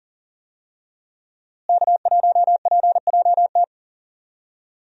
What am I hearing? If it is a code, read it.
Morse code, heard as a keyed tone, about 40 wpm, 700 Hz. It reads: K1PJT
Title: USA Call Signs - 5 Characters - Repeated 3x - 40wpm